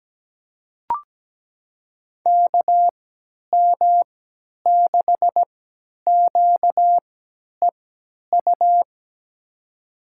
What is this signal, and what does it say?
Morse code, heard as a keyed tone, about 17 wpm, spaced as for 10 wpm, 700 Hz. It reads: KM6QEU
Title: USA Call Signs - 3x Character Spacing - 17wpm